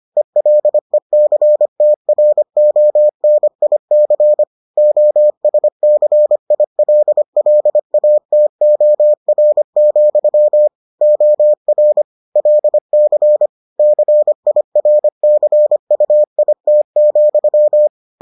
Morse code, heard as a keyed tone, about 25 wpm, 600 Hz. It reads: ELECTRONIC OSCILLATOR, OR LC CIRCUIT,